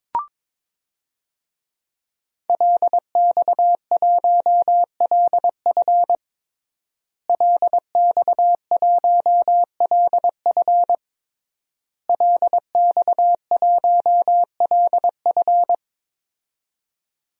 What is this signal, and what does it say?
Morse code, heard as a keyed tone, about 22 wpm, 700 Hz. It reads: LX1LF LX1LF LX1LF
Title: DXCC-100 Call Signs - 5 Characters - Repeated 3x - 22wpm